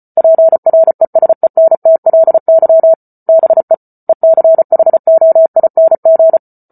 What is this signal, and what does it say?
Morse code, heard as a keyed tone, about 34 wpm, 650 Hz. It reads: PRESENTLY BE ECHOING